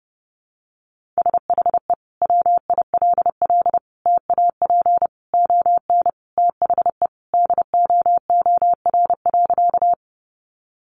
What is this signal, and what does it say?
Morse code, heard as a keyed tone, about 30 wpm, 700 Hz. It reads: SHE WILL TAP ON THE DOOR.
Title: Sentences from Top 1400 Words 30wpm